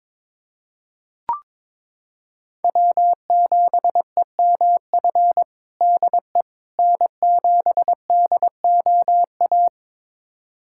Morse code, heard as a keyed tone, about 22 wpm, 700 Hz. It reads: W7EMF DE N7DOA
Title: USA Callsign DE Callsign - 5 Characters - 22wpm